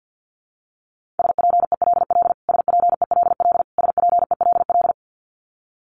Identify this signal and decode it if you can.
Morse code, heard as a keyed tone, about 50 wpm, 700 Hz. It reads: SPELL SPELL SPELL
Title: Top 500 Words Repeated 50wpm